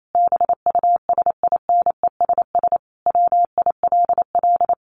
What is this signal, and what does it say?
Morse code, heard as a keyed tone, about 28 wpm, 700 Hz. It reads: BUSINESS WILL
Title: Sentences from Top 1400 Words 28wpm